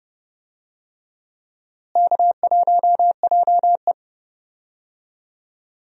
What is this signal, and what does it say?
Morse code, heard as a keyed tone, about 32 wpm, 700 Hz. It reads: K1JE
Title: USA Call Signs - 4 Characters - Repeated 3x - 30wpm